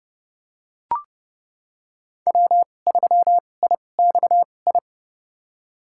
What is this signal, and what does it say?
Morse code, heard as a keyed tone, about 30 wpm, spaced as for 22 wpm, 700 Hz. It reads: W3IXI
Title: USA Call Signs - 5 Characters - 2x Character Spacing - 30wpm